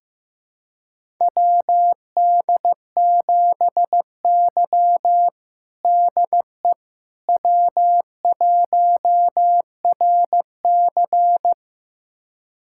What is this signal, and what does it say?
Morse code, heard as a keyed tone, about 16 wpm, 700 Hz. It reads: WD7Y DE W1RC
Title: USA Callsign DE Callsign - 4 Characters - 15wpm